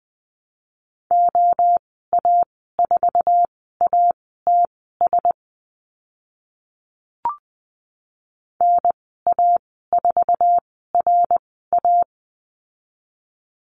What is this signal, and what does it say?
Morse code, heard as a keyed tone, about 20 wpm, spaced as for 14 wpm, 700 Hz. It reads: OA4ATS NA4RA
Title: 076 Lesson 13g - Letter L - Call Signs 20wpm